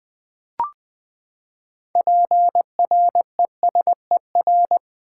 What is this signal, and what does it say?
Morse code, heard as a keyed tone, about 20 wpm, 700 Hz. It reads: PRESER
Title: Build A Word - 6 Letters At A Time - 12-Letter Words - 20wpm